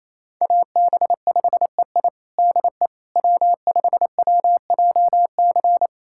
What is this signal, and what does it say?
Morse code, heard as a keyed tone, about 28 wpm, 700 Hz. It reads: AB5EI DE W5WJC
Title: USA Callsign DE Callsign - 5 Characters - 28wpm